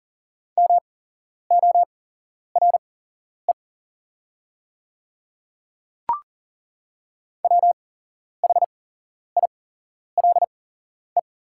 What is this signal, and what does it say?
Morse code, heard as a keyed tone, about 40 wpm, spaced as for 11 wpm, 690 Hz. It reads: MORE WHILE
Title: 8x Character Spacing - Top 300 Words - 40wpm